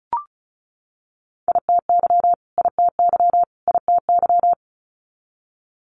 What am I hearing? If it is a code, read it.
Morse code, heard as a keyed tone, about 36 wpm, 700 Hz. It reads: ITY ITY ITY
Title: Multi-letter Phonemes Repeated 3x 35wpm